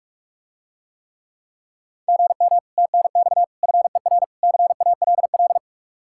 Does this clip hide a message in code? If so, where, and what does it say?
Yes: Morse code from 2.1 to 5.6 s, heard as a keyed tone, about 44 wpm, 700 Hz. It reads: GM TNX FER CALL